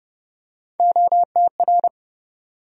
Morse code, heard as a keyed tone, about 30 wpm, 700 Hz. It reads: OTR
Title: USA Call Sign Postfixes - 3 Letters 30wpm